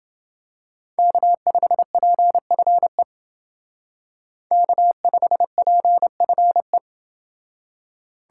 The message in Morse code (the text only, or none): K5PFE K5PFE